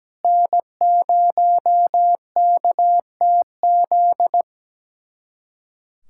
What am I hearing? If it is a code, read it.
Morse code, heard as a keyed tone, about 17 wpm, 700 Hz. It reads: N0KTZ
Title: USA Callsign DE Callsign - 5 Characters - 17wpm